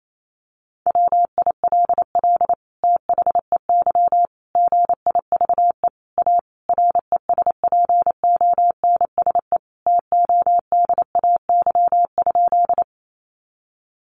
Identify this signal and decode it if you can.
Morse code, heard as a keyed tone, about 28 wpm, 700 Hz. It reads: WILL THEY GIVE A RESPONSE TODAY?
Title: Sentences from Top 1400 Words 28wpm